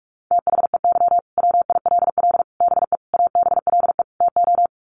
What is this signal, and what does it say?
Morse code, heard as a keyed tone, about 44 wpm, 700 Hz. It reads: THEY WILL BE ABLE TO
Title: Sets of 5 Words Encore 45wpm